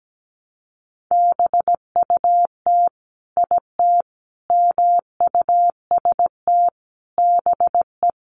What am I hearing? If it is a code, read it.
Morse code, heard as a keyed tone, about 17 wpm, 700 Hz. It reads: BUT IT MUST BE
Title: Sentences from Top 600 Words 17wpm